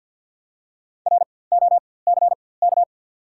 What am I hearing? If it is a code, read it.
Morse code, heard as a keyed tone, about 48 wpm, spaced as for 23 wpm, 700 Hz. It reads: ROCK
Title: 4x Character Spacing - Top 500 Words - 50wpm